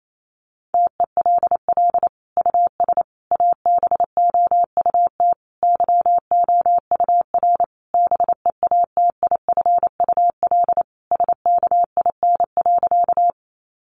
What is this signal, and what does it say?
Morse code, heard as a keyed tone, about 28 wpm, 700 Hz. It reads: TELL US ABOUT YOUR BEATIFUL SKIN.
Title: Sentences from Top 1300 Words 28wpm